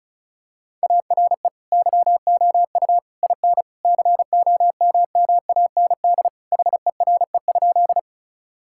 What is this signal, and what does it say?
Morse code, heard as a keyed tone, about 34 wpm, 700 Hz. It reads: ARE YOU IN COMMAND HERE?
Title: Sentences from Top 1600 Words 35wpm